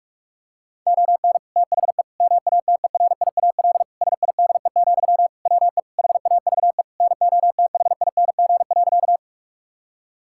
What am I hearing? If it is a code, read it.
Morse code, heard as a keyed tone, about 41 wpm, 700 Hz. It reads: ON THE MATERIAL SIDE, WE HAVE NOTHING.
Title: Sentences from Top 1000 Words - Review Part 5 - 45wpm